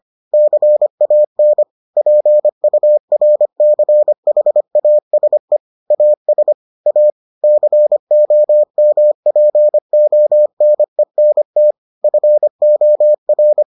Morse code, heard as a keyed tone, about 25 wpm, 600 Hz. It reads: CAN PURCHASE AS A COMPONENT FOR